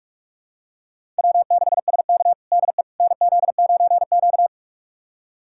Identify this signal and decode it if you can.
Morse code, heard as a keyed tone, about 46 wpm, 700 Hz. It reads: W6SK DE NZ9Q
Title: USA Callsign DE Callsign - 4 Characters - 45wpm